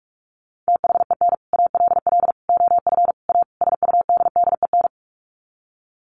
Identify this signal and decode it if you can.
Morse code, heard as a keyed tone, about 44 wpm, 700 Hz. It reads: THEN ALL OF A SUDDEN
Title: Sets of 5 Words Encore 45wpm